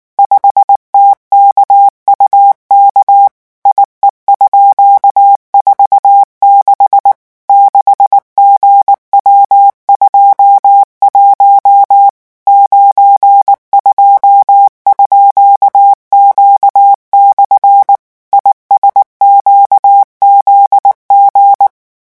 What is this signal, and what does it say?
Morse code, heard as a keyed tone, about 19 wpm, 800 Hz. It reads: 5TKUK IE_46 6GW21 92_Q/ ISQZG
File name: morse mp3 / revision12-vitesse 18 mots